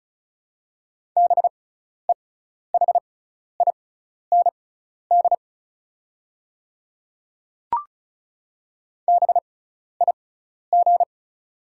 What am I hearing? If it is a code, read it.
Morse code, heard as a keyed tone, about 34 wpm, spaced as for 12 wpm, 690 Hz. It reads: BEHIND BIG
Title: 6x Character Spacing - Top 500 Words - 35wpm